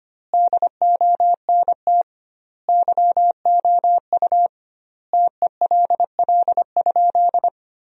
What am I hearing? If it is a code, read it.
Morse code, heard as a keyed tone, about 25 wpm, 700 Hz. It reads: DONT YOU TELL?
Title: 2x Word-Spacing - Sentences from Top 300 Words - 25wpm